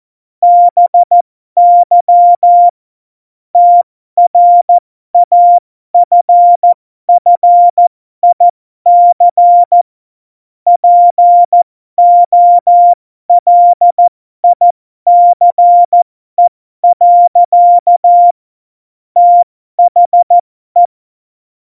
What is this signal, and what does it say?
Morse code, heard as a keyed tone, about 14 wpm, spaced as for 12 wpm, 700 Hz. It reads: BY TRAFFIC POLICE. THE
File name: Morsekurs / Witze